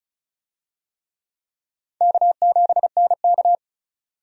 Morse code, heard as a keyed tone, about 36 wpm, 700 Hz. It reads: K7NK
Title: USA Call Signs - 4 Characters - 35wpm